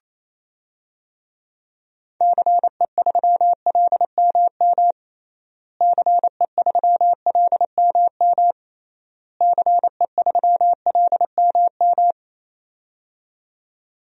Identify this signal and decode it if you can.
Morse code, heard as a keyed tone, about 28 wpm, 700 Hz. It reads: CE3LMM CE3LMM CE3LMM
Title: DXCC-100 Call Signs - 6 Characters - Repeated 3x - 28wpm